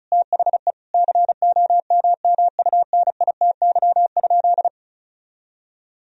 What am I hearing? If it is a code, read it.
Morse code, heard as a keyed tone, about 34 wpm, 700 Hz. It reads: THE COMMUNITY?
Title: Sentences from Top 1000 Words - Review Part 4 - 35wpm